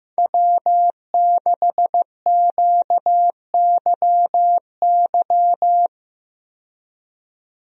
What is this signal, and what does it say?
Morse code, heard as a keyed tone, about 15 wpm, 700 Hz. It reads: W6QYY
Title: USA Call Signs - 5 Characters - 15wpm